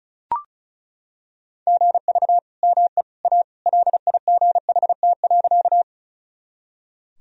Morse code, heard as a keyed tone, about 36 wpm, 700 Hz. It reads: GV ME A LIGHT.
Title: Sentences from Top 200 Words and Abbreviations 35wpm